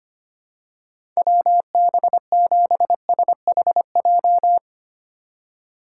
Morse code, heard as a keyed tone, about 25 wpm, 700 Hz. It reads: WB7SHJ
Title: USA Call Signs - 6 Characters - 4x Character Spacing - 25wpm